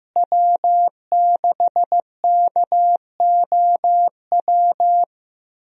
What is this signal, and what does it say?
Morse code, heard as a keyed tone, about 15 wpm, 700 Hz. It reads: W6KOW
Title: USA Call Signs - 5 Characters - Repeated 2x - 15wpm